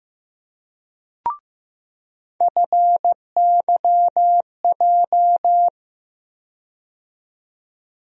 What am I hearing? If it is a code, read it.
Morse code, heard as a keyed tone, about 15 wpm, 700 Hz. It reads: FYJ
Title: USA Call Sign Postfixes - 3 Letters 15wpm